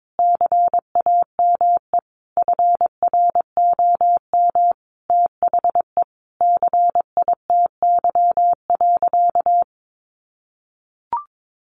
Morse code, heard as a keyed tone, about 22 wpm, 700 Hz. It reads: CAME FROM THE CITY.
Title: Sentences from Top 1500 Words 22wpm